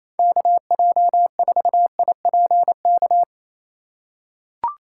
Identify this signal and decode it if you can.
Morse code, heard as a keyed tone, about 28 wpm, 700 Hz. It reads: KJ4IPK
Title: USA Call Signs - 6 Characters - 8x Character Spacing - 28wpm